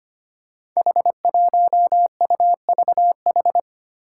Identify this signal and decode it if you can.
Morse code, heard as a keyed tone, about 25 wpm, 700 Hz. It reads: H1UVH
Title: Five Challenging Letter-Number Combinations 25wpm